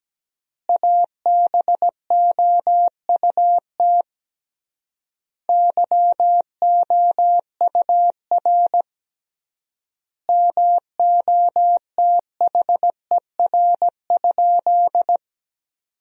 Morse code, heard as a keyed tone, about 17 wpm, 700 Hz. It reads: ABOUT YOUR MOTHER?
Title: 3x Word-Spacing - Sentences from Top 200 Words - 17wpm